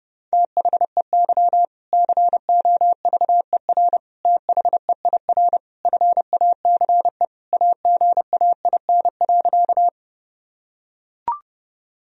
Morse code, heard as a keyed tone, about 30 wpm, 700 Hz. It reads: THEY COVER THEIR FACE AGAIN.
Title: Sentences from Top 1000 Words - Review Part 1 - 30wpm